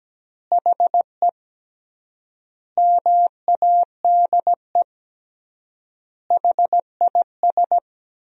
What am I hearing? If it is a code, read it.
Morse code, heard as a keyed tone, about 17 wpm, 700 Hz. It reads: HE MADE HIS
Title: 3x Word-Spacing - Sentences from Top 300 Words - 17wpm